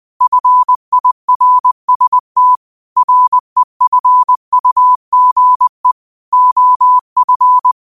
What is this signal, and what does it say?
Morse code, heard as a keyed tone, about 20 wpm, 1000 Hz. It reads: FIRST REFUGE OF
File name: qotdfiles / 20wpm